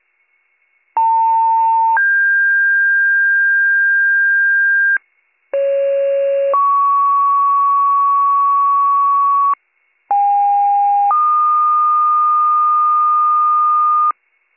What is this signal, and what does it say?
Two-tone page, heard as a one-second tone, then a three-second tone, about 900 then 1600 Hz, about 550 then 1120 Hz, about 810 then 1220 Hz. It is heard from 1.0 to 14.1 s.